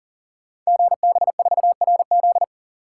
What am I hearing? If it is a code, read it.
Morse code, heard as a keyed tone, about 42 wpm, 690 Hz. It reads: GB4RZ
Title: Code Groups - 5 Letter:Numbers - 40wpm